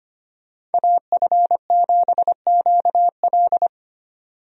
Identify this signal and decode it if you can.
Morse code, heard as a keyed tone, about 25 wpm, 700 Hz. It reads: AF7QL
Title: USA Call Signs - 5 Characters - 25wpm